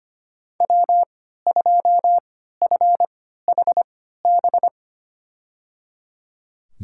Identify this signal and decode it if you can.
Morse code, heard as a keyed tone, about 25 wpm, spaced as for 14 wpm, 700 Hz. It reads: W2FHB
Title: USA Call Signs - 3x Character Spacing - 25wpm